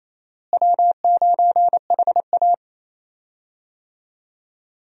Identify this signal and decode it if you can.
Morse code, heard as a keyed tone, about 28 wpm, 700 Hz. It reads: W9HA
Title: USA Call Signs - 4 Characters - Repeated 3x - 28wpm